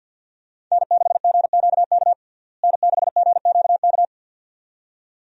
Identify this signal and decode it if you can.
Morse code, heard as a keyed tone, about 52 wpm, 700 Hz. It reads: N6GQK N6GQK
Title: USA Call Signs - 5 Characters - Repeated 2x - 50wpm